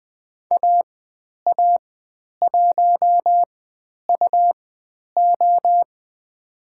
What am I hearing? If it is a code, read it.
Morse code, heard as a keyed tone, about 20 wpm, spaced as for 10 wpm, 700 Hz. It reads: AA1UO
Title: 102 Lesson 17g - Letter u - Call Signs 20wpm